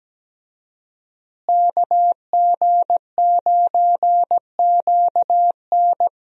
Morse code, heard as a keyed tone, about 17 wpm, 700 Hz. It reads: KG9QN
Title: USA Call Signs - 5 Characters - Repeated 2x - 17wpm